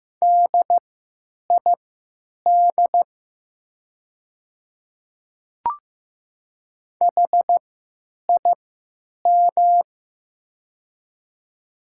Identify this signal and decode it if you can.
Morse code, heard as a keyed tone, about 15 wpm, spaced as for 9 wpm, 700 Hz. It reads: DID HIM
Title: 3x Character Spacing - Top 100 Words - 15wpm